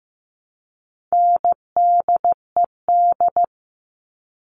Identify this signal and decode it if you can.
Morse code, heard as a keyed tone, about 15 wpm, 700 Hz. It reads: NDED
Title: All Cut Numbers - 4-Digit 15wpm